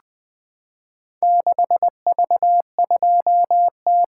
Morse code, heard as a keyed tone, about 20 wpm, 700 Hz. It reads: 6V2T
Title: Four Letter-Number Combinations - 20wpm